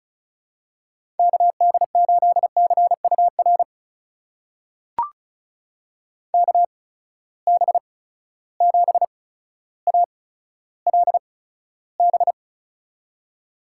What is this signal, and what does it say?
Morse code, heard as a keyed tone, about 36 wpm, spaced as for 15 wpm, 700 Hz. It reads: KD8CUR KB7ALB
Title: USA Call Signs - 6 Characters - 8x Character Spacing - 35wpm